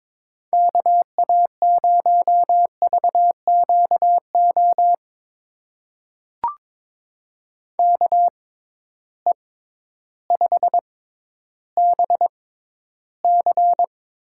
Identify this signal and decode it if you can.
Morse code, heard as a keyed tone, about 22 wpm, spaced as for 12 wpm, 700 Hz. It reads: KA0VQO KE5BC
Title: USA Call Signs - 6 Characters - 6x Character Spacing - 22wpm